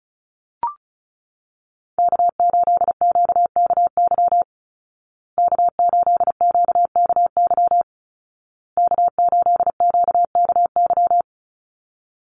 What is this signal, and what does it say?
Morse code, heard as a keyed tone, about 34 wpm, 700 Hz. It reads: K8QKY K8QKY K8QKY
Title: Call Signs Repeated 2x - 35wpm